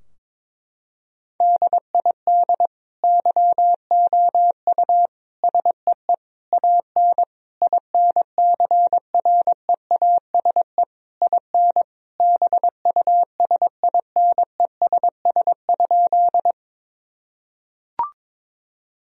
Morse code, heard as a keyed tone, about 22 wpm, 700 Hz. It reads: DID YOU SEE AN INCREASE IN BUSINESS?
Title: Sentences from Top 1000 Words - Review Part 4 - 22wpm